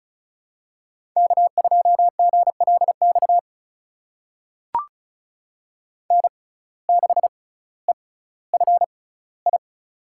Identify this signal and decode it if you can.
Morse code, heard as a keyed tone, about 36 wpm, spaced as for 18 wpm, 700 Hz. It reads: K2GLX N6EFI
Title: USA Call Signs - 5 Characters - 6x Character Spacing - 35wpm